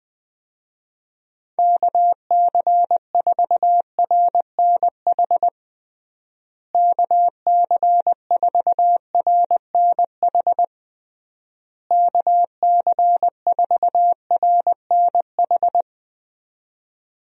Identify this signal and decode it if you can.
Morse code, heard as a keyed tone, about 20 wpm, 700 Hz. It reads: KC4RNH KC4RNH KC4RNH
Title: USA Call Signs - 6 Characters - Repeated 3x - 20wpm